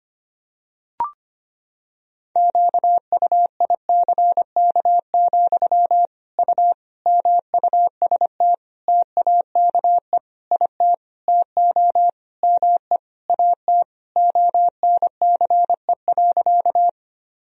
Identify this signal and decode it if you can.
Morse code, heard as a keyed tone, about 25 wpm, 700 Hz. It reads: QUICK, U MUST TAKE IT TO ME AT ONCE.
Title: Sentences from Top 500 Words and Abbreviations 25wpm